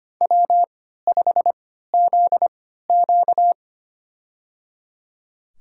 Morse code, heard as a keyed tone, about 25 wpm, spaced as for 14 wpm, 700 Hz. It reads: W5ZQ
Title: USA Call Signs - 4 Characters - 3x Character Spacing - 25wpm